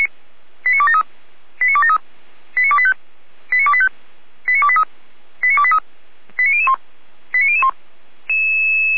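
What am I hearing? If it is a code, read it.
Five-tone selective call (ZVEI), heard as quick runs of five tones, tones of about 70 ms. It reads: 89273 89273 89285 89285 89283 89283 89001 89001
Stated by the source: ZVEI five-tone selective call, address 89503, received on an SDR.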